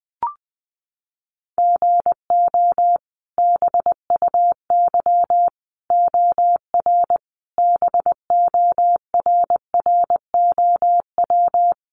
Morse code, heard as a keyed tone, about 20 wpm, 700 Hz. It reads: GO BUY OR BORROW
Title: Sentences from Top 1500 Words 20wpm